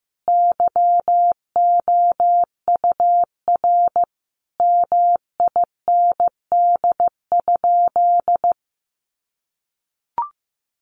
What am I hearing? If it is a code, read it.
Morse code, heard as a keyed tone, about 15 wpm, 700 Hz. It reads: YOUR MIND?
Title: Sentences from Top 1400 Words 15wpm